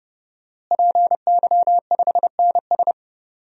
Morse code, heard as a keyed tone, about 32 wpm, 700 Hz. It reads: PY5NS
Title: DXCC-100 Call Signs - 5 Characters - Repeated 3x - 30wpm